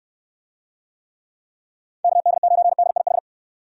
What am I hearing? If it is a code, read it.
Morse code, heard as a keyed tone, about 69 wpm, 700 Hz. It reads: KD9DIH